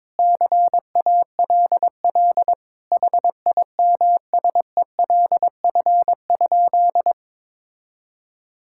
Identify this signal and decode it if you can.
Morse code, heard as a keyed tone, about 22 wpm, 700 Hz. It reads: CALL HIMSELF?
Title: Sentences from Top 1600 Words 22wpm